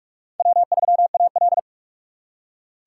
Morse code, heard as a keyed tone, about 46 wpm, 700 Hz. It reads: W3AL
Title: USA Call Signs - 4 Characters - 3x Character Spacing - 45wpm